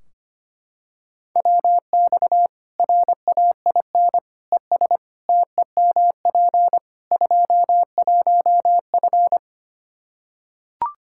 Morse code, heard as a keyed tone, about 25 wpm, 700 Hz. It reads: WX RAIN ES TEMP 21F
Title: Practice QSO - Exchange 2 - Info - 25wpm